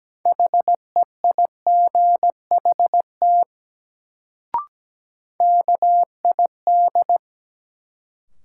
Morse code, heard as a keyed tone, about 17 wpm, 700 Hz. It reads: HEIGHT KID
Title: Sentences from Top 1600 Words - New Word Drill Alt - 17wpm